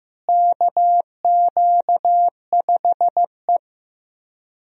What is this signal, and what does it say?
Morse code, heard as a keyed tone, about 15 wpm, 700 Hz. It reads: KQ5E
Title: USA Call Signs - 4 Characters - Repeated 3x - 15wpm